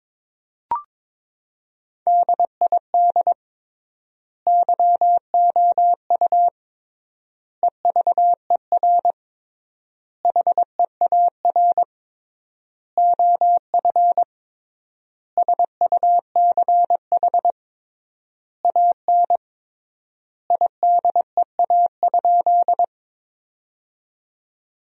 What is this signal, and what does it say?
Morse code, heard as a keyed tone, about 22 wpm, 700 Hz. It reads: DID YOU EVER HEAR OF SUCH AN IDEA?
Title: 3x Word-Spacing - Sentences from Top 400 Words - 22wpm